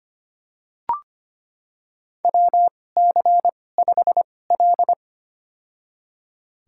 Morse code, heard as a keyed tone, about 25 wpm, spaced as for 18 wpm, 700 Hz. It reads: WC5L
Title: USA Call Signs - 4 Characters - 2x Character Spacing - 25wpm